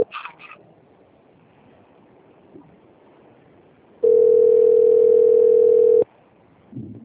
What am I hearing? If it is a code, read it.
Ringback tone (North America), 440 + 480 Hz.
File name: rg-1600-unknown- - - .8085